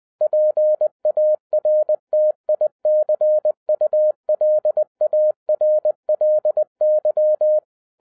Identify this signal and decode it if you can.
Morse code, heard as a keyed tone, about 20 wpm, 610 Hz. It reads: PARTICULARLY